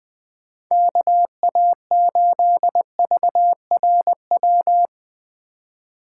Morse code, heard as a keyed tone, about 20 wpm, 700 Hz. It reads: KA8VRW
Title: USA Call Signs - 6 Characters - 3x Character Spacing - 20wpm